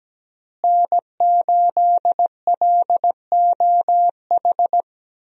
Morse code, heard as a keyed tone, about 17 wpm, 700 Hz. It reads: N8LOH
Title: USA Call Signs - 5 Characters - Repeated 2x - 17wpm